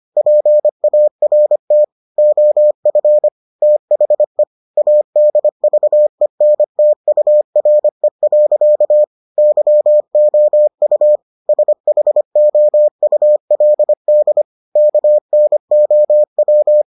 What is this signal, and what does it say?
Morse code, heard as a keyed tone, about 25 wpm, 600 Hz. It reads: PART OF THE ADVENTURE. YOU SHOULD KNOW